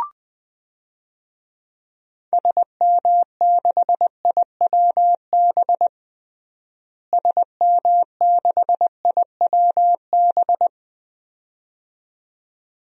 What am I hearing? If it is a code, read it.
Morse code, heard as a keyed tone, about 21 wpm, 700 Hz. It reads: SM6IWB SM6IWB
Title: DXCC-100 Call Signs - 6 Characters - Repeated 2x - 20wpm